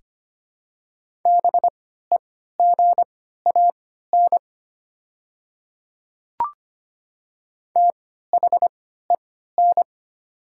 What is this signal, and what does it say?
Morse code, heard as a keyed tone, about 25 wpm, spaced as for 14 wpm, 700 Hz. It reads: BEGAN THEN
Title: 3x Character Spacing - Top 300 Words - 25wpm